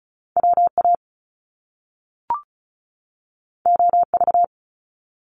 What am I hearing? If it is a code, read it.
Morse code, heard as a keyed tone, about 36 wpm, 700 Hz. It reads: WA OV
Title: Top Two Letter Combinations in Words 35wpm